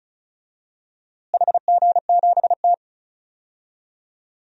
Morse code, heard as a keyed tone, about 34 wpm, 700 Hz. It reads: HG7T
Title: DXCC-100 Call Signs - 4 Characters - Repeated 2x - 35wpm